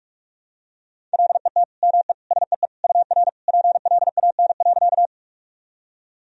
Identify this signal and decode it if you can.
Morse code, heard as a keyed tone, about 46 wpm, 700 Hz. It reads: LET ME SEE UR PLAN.
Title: Sentences from Top 500 Words and Abbreviations 45wpm